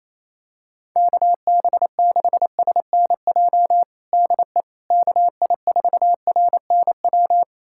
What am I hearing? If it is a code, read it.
Morse code, heard as a keyed tone, about 28 wpm, 700 Hz. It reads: KB6SNJ DE KI4RNW
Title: USA Callsign DE Callsign - 6 Characters - 28wpm